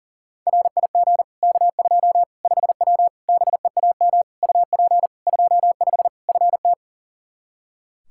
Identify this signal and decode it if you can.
Morse code, heard as a keyed tone, about 40 wpm, 700 Hz. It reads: RIG K2 5W BEAM UP 25 FT
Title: Practice QSO - Exchange 2 - Info - 40wpm